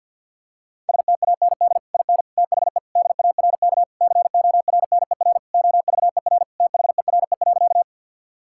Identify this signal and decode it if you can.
Morse code, heard as a keyed tone, about 50 wpm, 700 Hz. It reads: STAND IN THE DARK CORNER OVER THERE.
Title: Sentences from Top 1600 Words 50wpm